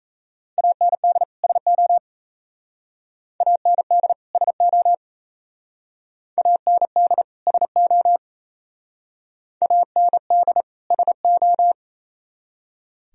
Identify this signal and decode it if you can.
Morse code, heard as a keyed tone, about 37 wpm, 700 Hz. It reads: AND SO AND SO AND SO AND SO